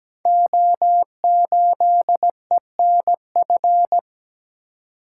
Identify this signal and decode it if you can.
Morse code, heard as a keyed tone, about 17 wpm, 700 Hz. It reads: O8ENF
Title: Code Groups - 5 Letter:Numbers - 17wpm